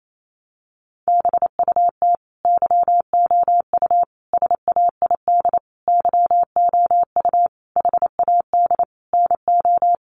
Morse code, heard as a keyed tone, about 28 wpm, 700 Hz. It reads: BUT YOU SAID YOU HAD NO
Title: Sentences from Top 1400 Words 28wpm